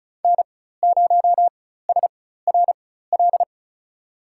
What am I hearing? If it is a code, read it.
Morse code, heard as a keyed tone, about 34 wpm, spaced as for 16 wpm, 700 Hz. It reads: N0SRL
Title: USA Call Signs - 5 Characters - 4x Character Spacing - 35wpm